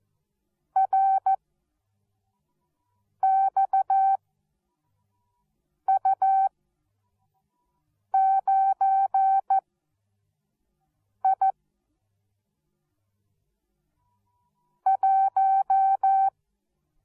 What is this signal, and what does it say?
Morse code, heard as a keyed tone, about 14 wpm, 770 Hz. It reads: R X U 9 I 1